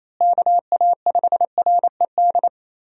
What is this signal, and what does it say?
Morse code, heard as a keyed tone, about 28 wpm, 700 Hz. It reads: KA5RED
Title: USA Call Signs - 6 Characters - Repeated 3x - 28wpm